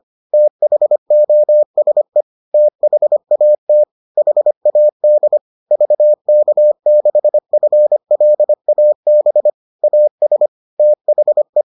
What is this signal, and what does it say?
Morse code, heard as a keyed tone, about 25 wpm, 600 Hz. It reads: THOSE THAT HAD VK6FLAB AS THE